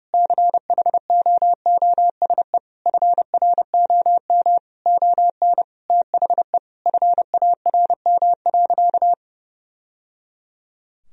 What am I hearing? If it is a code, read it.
Morse code, heard as a keyed tone, about 30 wpm, 700 Hz. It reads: CHOOSE FROM ON THE FARM.
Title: Sentences from Top 1000 Words - Review Part 4 - 30wpm